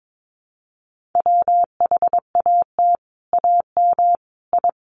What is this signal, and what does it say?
Morse code, heard as a keyed tone, about 22 wpm, 700 Hz. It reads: WHAT AM I
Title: Sets of 5 Words Encore 22wpm